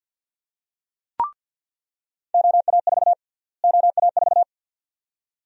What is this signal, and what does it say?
Morse code, heard as a keyed tone, about 46 wpm, 700 Hz. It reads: OA4 OA4
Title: USA Call Sign Prefixes - Repeated 2x - 50wpm